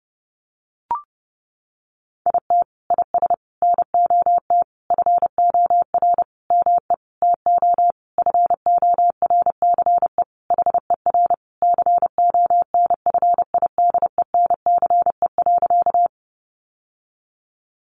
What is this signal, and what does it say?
Morse code, heard as a keyed tone, about 30 wpm, 700 Hz. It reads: IT IS NOT FOR ME TO FORCE HER CONFIDENCE.
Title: Sentences from Top 1500 Words 30wpm